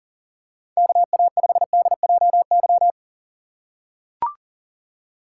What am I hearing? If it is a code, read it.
Morse code, heard as a keyed tone, about 40 wpm, 700 Hz. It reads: KA5DJY
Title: USA Call Signs - 6x Character Spacing - 40wpm